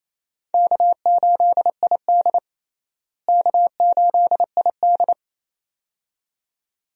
Morse code, heard as a keyed tone, about 28 wpm, 700 Hz. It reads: K8ID K8ID
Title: USA Call Signs - 4 Characters - Repeated 2x - 28wpm